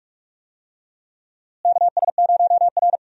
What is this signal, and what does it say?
Morse code, heard as a keyed tone, about 47 wpm, 700 Hz. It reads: KS0R
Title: USA Call Signs - 4 Characters - Repeated 2x - 45wpm